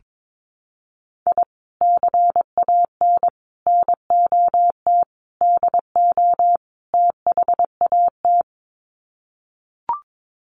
Morse code, heard as a keyed tone, about 22 wpm, 700 Hz. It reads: I CAN NOT DO THAT
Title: Sets of 5 Words Encore2 22wpm